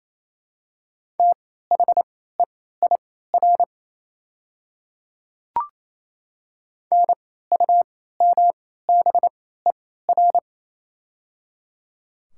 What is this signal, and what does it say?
Morse code, heard as a keyed tone, about 28 wpm, spaced as for 16 wpm, 700 Hz. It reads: THEIR NUMBER
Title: 3x Character Spacing - Top 200 Words - 28wpm